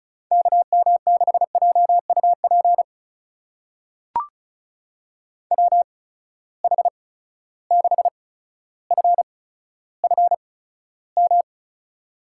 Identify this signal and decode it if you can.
Morse code, heard as a keyed tone, about 36 wpm, spaced as for 15 wpm, 700 Hz. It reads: KM6JUP WH6FFM
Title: USA Call Signs - 6 Characters - 8x Character Spacing - 35wpm